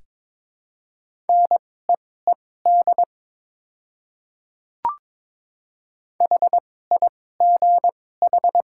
Morse code, heard as a keyed tone, about 21 wpm, spaced as for 16 wpm, 700 Hz. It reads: NEED HIGH
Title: 2x Character Spacing - Top 200 Words - 22wpm